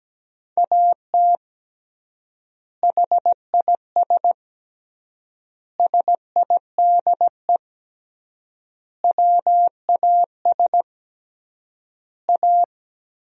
Morse code, heard as a keyed tone, about 17 wpm, 700 Hz. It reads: AT HIS SIDE WAS A
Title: 3x Word-Spacing - Sentences from Top 500 Words - 17wpm